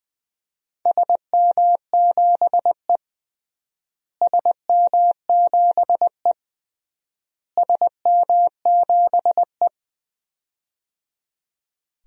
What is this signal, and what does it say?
Morse code, heard as a keyed tone, about 21 wpm, 700 Hz. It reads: SM7E SM7E SM7E
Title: DXCC-100 Call Signs - 4 Characters - Repeated 3x - 20wpm